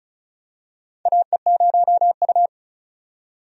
Morse code, heard as a keyed tone, about 36 wpm, 700 Hz. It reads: AE0U
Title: USA Call Signs - 4 Characters - 35wpm